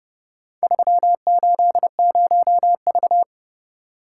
Morse code, heard as a keyed tone, about 28 wpm, 700 Hz. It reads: 380V